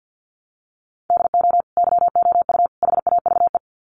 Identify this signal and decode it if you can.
Morse code, heard as a keyed tone, about 50 wpm, 700 Hz. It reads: DO YOU HAVE
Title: Sets of 3 Words 50wpm